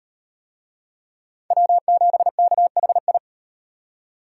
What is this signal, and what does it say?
Morse code, heard as a keyed tone, about 38 wpm, 700 Hz. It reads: W7KHI